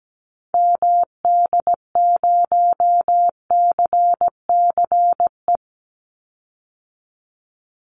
Morse code, heard as a keyed tone, about 17 wpm, 700 Hz. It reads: MD0CCE
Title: Call Signs Repeated 2x - 17wpm